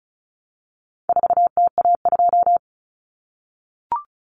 Morse code, heard as a keyed tone, about 34 wpm, 700 Hz. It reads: 4TA2
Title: Common Cut Numbers - 4-Digit 35wpm